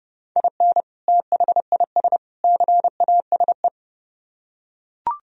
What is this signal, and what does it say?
Morse code, heard as a keyed tone, about 28 wpm, 700 Hz. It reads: IN THIS CASE